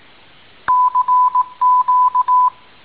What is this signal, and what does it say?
Morse code, heard as a keyed tone, about 18 wpm, 1000 Hz. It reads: CQ